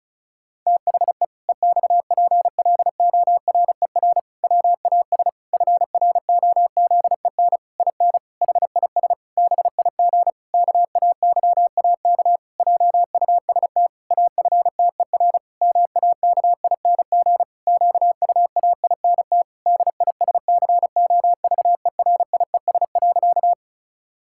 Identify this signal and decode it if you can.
Morse code, heard as a keyed tone, about 36 wpm, 700 Hz. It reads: THE EXPLORER WAS FROZEN IN HIS BIG KAYAK JUST AFTER MAKING QUAINT DISCOVERIES.
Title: Pangram Challenge v2 - 35wpm